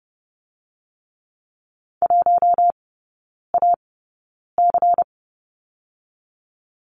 Morse code, heard as a keyed tone, about 30 wpm, spaced as for 9 wpm, 700 Hz. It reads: 1AC
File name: Lesson 17e - Letter u - Three Characters 30wpm